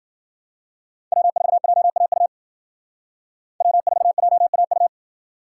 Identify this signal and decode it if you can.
Morse code, heard as a keyed tone, about 48 wpm, 700 Hz. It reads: W4JAU W4JAU
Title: Speed Racing - USA Call Signs 40wpm